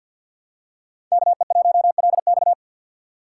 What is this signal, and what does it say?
Morse code, heard as a keyed tone, about 52 wpm, 700 Hz. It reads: KE1LX
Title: USA Call Signs - 5 Characters - 4x Character Spacing - 50wpm